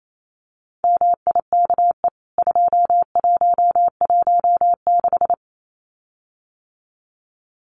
Morse code, heard as a keyed tone, about 28 wpm, 700 Hz. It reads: MIKE 2116
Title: CWT Exchange 28wpm